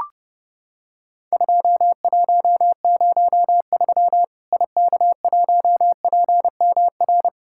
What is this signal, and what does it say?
Morse code, heard as a keyed tone, about 30 wpm, 700 Hz. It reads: 2103 IK1PMR
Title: Worked All Europe DX Contest - QTC Exchange - 30wpm